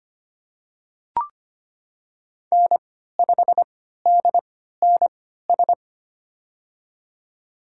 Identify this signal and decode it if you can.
Morse code, heard as a keyed tone, about 25 wpm, spaced as for 14 wpm, 700 Hz. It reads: N5DNS
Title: USA Call Signs - 3x Character Spacing - 25wpm